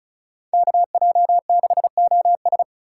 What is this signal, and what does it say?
Morse code, heard as a keyed tone, about 36 wpm, 700 Hz. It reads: KJ6OS
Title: USA Call Signs - 5 Characters - 2x Character Spacing - 35wpm